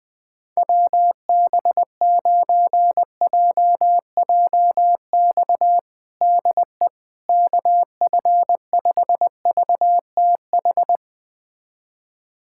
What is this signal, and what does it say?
Morse code, heard as a keyed tone, about 20 wpm, 700 Hz. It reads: WB9JJX DE KF5VTH
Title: USA Callsign DE Callsign - 6 Characters - 20wpm